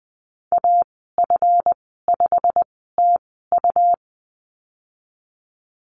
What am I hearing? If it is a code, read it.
Morse code, heard as a keyed tone, about 20 wpm, spaced as for 15 wpm, 700 Hz. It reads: AF5TU
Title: 154 Lesson 25g - Letter p - Call Signs 20wpm